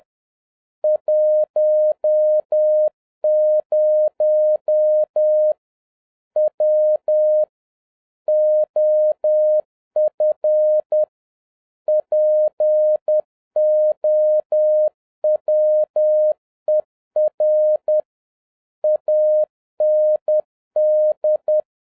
Morse code, heard as a keyed tone, about 10 wpm, 610 Hz. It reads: 10 W OF POWER AND